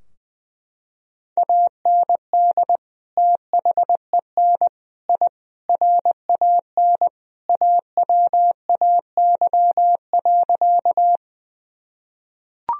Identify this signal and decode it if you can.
Morse code, heard as a keyed tone, about 20 wpm, 700 Hz. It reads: AND THEN I RAN AWAY.
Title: Sentences from Top 1000 Words - Review Part 3 - 20wpm